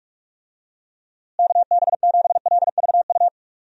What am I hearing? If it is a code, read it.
Morse code, heard as a keyed tone, about 46 wpm, 700 Hz. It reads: KB7LVU
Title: USA Call Signs - 6 Characters - Repeated 3x - 45wpm